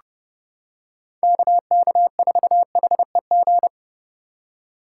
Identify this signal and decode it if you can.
Morse code, heard as a keyed tone, about 31 wpm, 700 Hz. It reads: KK4HEG
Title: K1USN SST QSO Practice - 30wpm